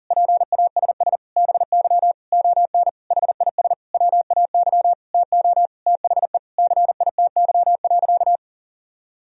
Morse code, heard as a keyed tone, about 37 wpm, 700 Hz. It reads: PASS BY ON HIS WAY TO THE CITY.
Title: Sentences from Top 1000 Words - Review Part 1 - 40wpm